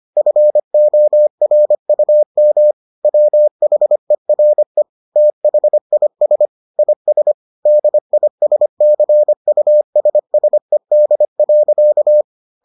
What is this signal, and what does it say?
Morse code, heard as a keyed tone, about 25 wpm, 600 Hz. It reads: FORUM WHERE THIS IS DISCUSSED.